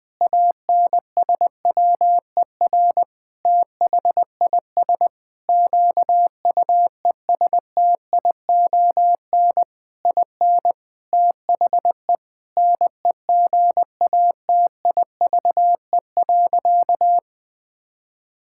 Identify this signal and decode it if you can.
Morse code, heard as a keyed tone, about 20 wpm, 700 Hz. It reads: ANSWER THIS QUESTION IN THE NEGATIVE.
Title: Sentences from Top 1000 Words - Review Part 6 - 20wpm